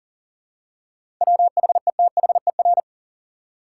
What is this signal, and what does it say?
Morse code, heard as a keyed tone, about 40 wpm, 700 Hz. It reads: WHETHER
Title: Seven Letter Words - 40wpm